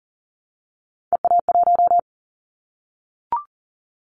Morse code, heard as a keyed tone, about 40 wpm, 700 Hz. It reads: EA1
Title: Callsign Prefixes for Top 25 DXCC - Repeated 3x - 40wpm